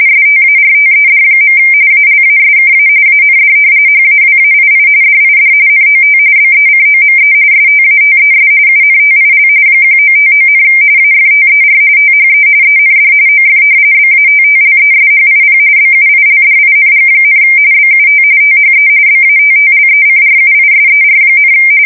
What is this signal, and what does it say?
RTTY, about 45.45 baud. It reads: STARTED. ALASKA AND HAWAII DID NOT BECOME STATES UNTIL 1959, AFTER THE END OF BATTLESHIP BUILDING, BUT THE BATTLECRUISER,
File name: fsk170 / battleships